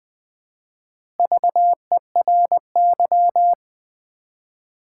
Morse code, heard as a keyed tone, about 20 wpm, 700 Hz. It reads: VERY
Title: 4x Word-Spacing - Sets of 3 Words - 20wpm